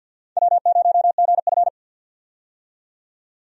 Morse code, heard as a keyed tone, about 51 wpm, 700 Hz. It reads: W0GF
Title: USA Call Signs - 4 Characters - 50wpm